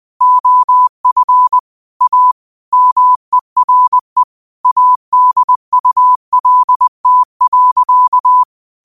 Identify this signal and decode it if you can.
Morse code, heard as a keyed tone, about 20 wpm, 1000 Hz. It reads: OF A MERE ADULT.
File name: qotdfiles / 20wpm